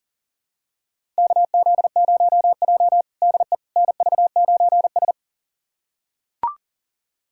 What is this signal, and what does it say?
Morse code, heard as a keyed tone, about 40 wpm, 700 Hz. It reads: KZ0J DE NV9S
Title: USA Callsign DE Callsign - 4 Characters - 40wpm